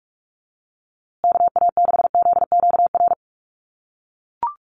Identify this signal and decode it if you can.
Morse code, heard as a keyed tone, about 44 wpm, 700 Hz. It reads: KA6ZQR
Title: USA Call Signs Repeated 3x 45wpm